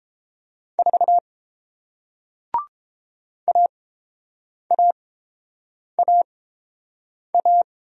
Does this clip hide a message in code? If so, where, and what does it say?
Yes: Morse code from 0.8 to 7.6 s, heard as a keyed tone, about 31 wpm, spaced as for 7 wpm, 700 Hz. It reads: 4 AAAA